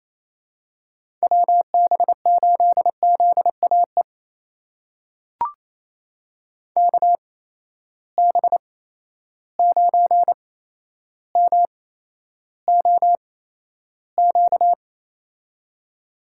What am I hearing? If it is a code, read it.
Morse code, heard as a keyed tone, about 28 wpm, spaced as for 12 wpm, 700 Hz. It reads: WB8ZAE KB9MOQ
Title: USA Call Signs - 6 Characters - 8x Character Spacing - 28wpm